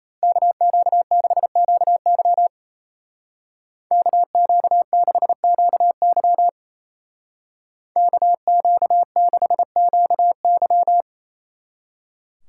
Morse code, heard as a keyed tone, about 33 wpm, 700 Hz. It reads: KQ6QY KQ6QY KQ6QY
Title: Speed Racing - USA 2x2 Call Signs 28wpm